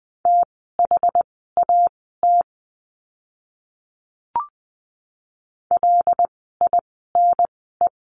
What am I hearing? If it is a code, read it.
Morse code, heard as a keyed tone, about 20 wpm, spaced as for 14 wpm, 700 Hz. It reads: THAT LINE
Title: 115 Lesson 19f - Letter w - Words 20wpm